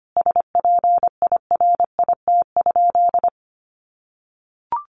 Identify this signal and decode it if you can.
Morse code, heard as a keyed tone, about 25 wpm, 700 Hz. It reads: SPIRIT?
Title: Sentences from Top 1400 Words 25wpm